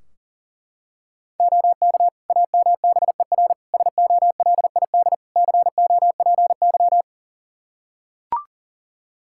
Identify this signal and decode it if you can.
Morse code, heard as a keyed tone, about 40 wpm, 700 Hz. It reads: OK AMBER SOLID COPY
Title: Practice QSO - Exchange 3 - More Info - 40wpm